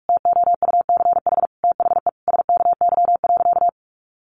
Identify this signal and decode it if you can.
Morse code, heard as a keyed tone, about 44 wpm, 700 Hz. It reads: TOUCH THE SKY.
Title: Additional Easy Phrases 45wpm